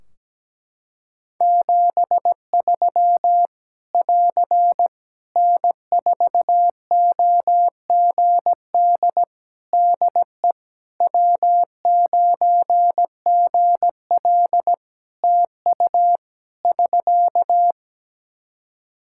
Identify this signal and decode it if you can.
Morse code, heard as a keyed tone, about 17 wpm, 700 Hz. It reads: 73 <AR> N4OGD DE W9GL TU <SK>
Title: K1USN SST QSO Practice 17wpm